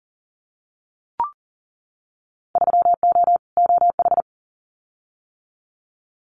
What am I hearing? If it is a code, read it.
Morse code, heard as a keyed tone, about 40 wpm, 700 Hz. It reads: 3O OH
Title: Winter Field Day Exchanges 40wpm